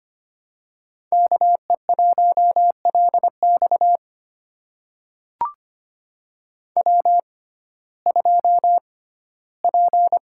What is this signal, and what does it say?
Morse code, heard as a keyed tone, about 25 wpm, spaced as for 15 wpm, 700 Hz. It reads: KE1LX W2P
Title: USA Call Signs - 5 Characters - 6x Character Spacing - 25wpm